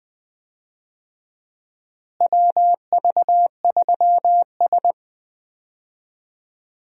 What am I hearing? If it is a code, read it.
Morse code, heard as a keyed tone, about 20 wpm, 700 Hz. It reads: WV3S
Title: USA Call Signs - 4 Characters - Repeated 3x - 20wpm